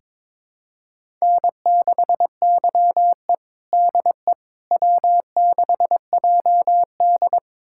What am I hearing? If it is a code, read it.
Morse code, heard as a keyed tone, about 22 wpm, 700 Hz. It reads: N6YE DE W6JD
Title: USA Callsign DE Callsign - 4 Characters - 22wpm